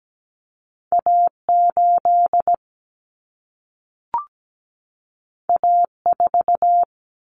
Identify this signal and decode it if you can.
Morse code, heard as a keyed tone, about 17 wpm, 700 Hz. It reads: A8 A4
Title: Common Cut Numbers - 2-Digit 17wpm